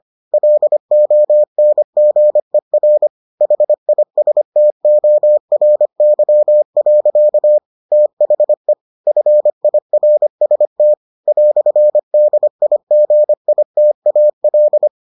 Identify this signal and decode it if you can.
Morse code, heard as a keyed tone, about 25 wpm, 600 Hz. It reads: LONGER HISTORY. THE FIRST "DIGITAL